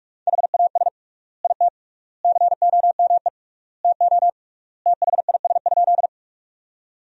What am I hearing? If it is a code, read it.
Morse code, heard as a keyed tone, about 43 wpm, 700 Hz. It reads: HAS IT COME TO THIS?
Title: 3x Word-Spacing - Sentences from Top 100 Words - 45wpm